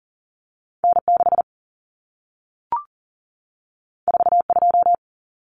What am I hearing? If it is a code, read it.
Morse code, heard as a keyed tone, about 40 wpm, 700 Hz. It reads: N6 42
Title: Common Cut Numbers - 2-Digit 40wpm